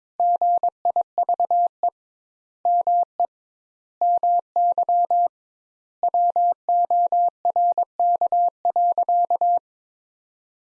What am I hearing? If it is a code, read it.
Morse code, heard as a keyed tone, about 22 wpm, 700 Hz. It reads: GIVE ME MY WORK.
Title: Sentences Top 200 Words - 22wpm